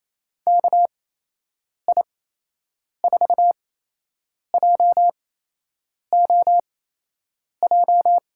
Morse code, heard as a keyed tone, about 28 wpm, spaced as for 8 wpm, 700 Hz. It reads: KI4JOJ
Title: USA Call Signs - 6 Characters - 8x Character Spacing - 28wpm